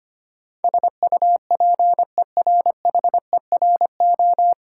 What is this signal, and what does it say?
Morse code, heard as a keyed tone, about 25 wpm, 700 Hz. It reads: SUPERHERO
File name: practice / 25wpm / Difficult Words with Hints - 25wpm